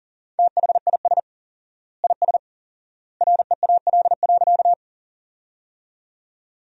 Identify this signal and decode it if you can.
Morse code, heard as a keyed tone, about 40 wpm, 700 Hz. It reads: THIS IS REAL.
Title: 4x Word-Spacing - Sentences from Top 300 Words - 40wpm